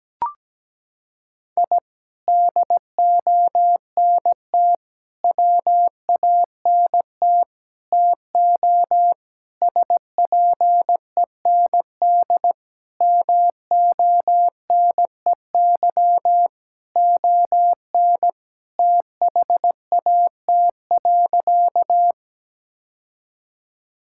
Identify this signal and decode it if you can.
Morse code, heard as a keyed tone, about 17 wpm, 700 Hz. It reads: I DONT WANT TO SPEND MONEY ON THAT.
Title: Sentences from Top 1000 Words - Review Part 5 - 17wpm